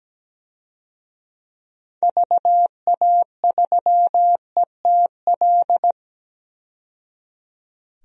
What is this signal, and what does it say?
Morse code, heard as a keyed tone, about 17 wpm, 700 Hz. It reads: VA3ETL
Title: DXCC-100 Call Signs - 6 Characters - 17wpm